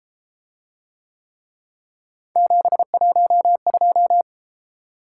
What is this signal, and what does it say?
Morse code, heard as a keyed tone, about 34 wpm, 700 Hz. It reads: 712